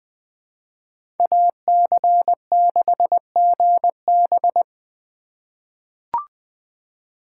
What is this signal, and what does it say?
Morse code, heard as a keyed tone, about 20 wpm, 700 Hz. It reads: AC6GB
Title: USA Call Signs - 5 Characters - 3x Character Spacing - 20wpm